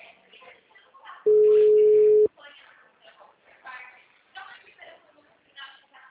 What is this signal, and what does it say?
Ringback tone (Europe, 425 Hz), with speech.